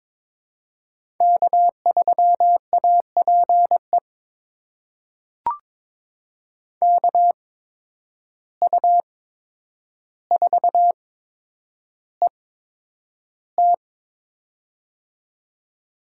Morse code, heard as a keyed tone, about 21 wpm, spaced as for 9 wpm, 700 Hz. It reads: K3APE KU4ET
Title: USA Call Signs - 5 Characters - 8x Character Spacing - 22wpm